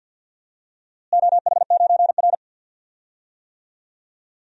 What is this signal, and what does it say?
Morse code, heard as a keyed tone, about 52 wpm, 700 Hz. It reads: OH9R
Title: DXCC-100 Call Signs - 4 Characters - Repeated 3x - 50wpm